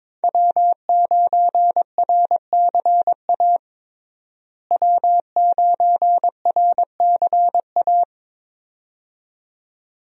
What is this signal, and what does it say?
Morse code, heard as a keyed tone, about 22 wpm, 700 Hz. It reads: W9RCA W9RCA
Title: USA Call Signs - 5 Characters - Repeated 2x - 22wpm